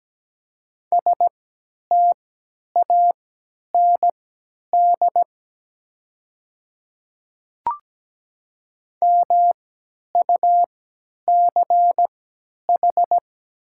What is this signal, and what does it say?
Morse code, heard as a keyed tone, about 17 wpm, spaced as for 10 wpm, 700 Hz. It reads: STAND MUCH
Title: 3x Character Spacing - Top 500 Words - 17wpm